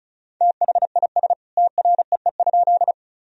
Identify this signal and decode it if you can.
Morse code, heard as a keyed tone, about 34 wpm, 700 Hz. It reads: THIS TREE?
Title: Sentences from Top 1000 Words - Review Part 1 - 35wpm